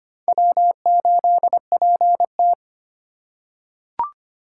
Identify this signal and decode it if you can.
Morse code, heard as a keyed tone, about 25 wpm, 700 Hz. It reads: W8PT
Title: USA Call Signs - 2x Character Spacing - 25wpm